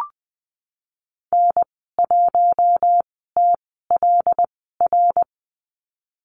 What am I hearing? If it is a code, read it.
Morse code, heard as a keyed tone, about 20 wpm, spaced as for 15 wpm, 700 Hz. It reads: N1TLR
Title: 122 Lesson 20g - Number 3 - Call Signs 20wpm